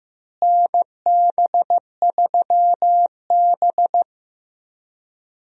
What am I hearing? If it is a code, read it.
Morse code, heard as a keyed tone, about 16 wpm, 700 Hz. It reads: NB3B
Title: USA Call Signs - 4 Characters - Repeated 3x - 15wpm